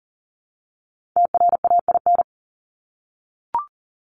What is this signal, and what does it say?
Morse code, heard as a keyed tone, about 40 wpm, 700 Hz. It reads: TRAIN
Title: ING Suffix - Top 100 Words with Base Words 40wpm